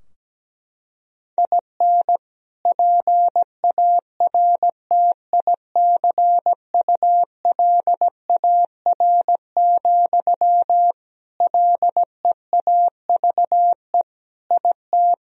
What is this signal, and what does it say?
Morse code, heard as a keyed tone, about 17 wpm, 700 Hz. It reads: IN PARTICULAR, LEAVE IT
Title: Sentences from Top 1000 Words - Review Part 4 - 17wpm